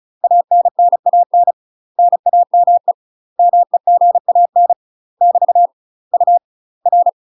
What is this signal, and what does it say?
Morse code, heard as a keyed tone, about 34 wpm, 700 Hz. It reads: ANNAN NAME MEGAN <BT> U R
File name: UK MASTER QSOs TRAINING / 35WPM / UK MASTER QSOs 0014